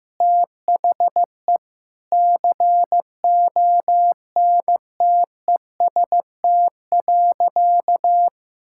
Morse code, heard as a keyed tone, about 15 wpm, 700 Hz. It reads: THE CONTEST.
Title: Sentences from Top 1600 Words 15wpm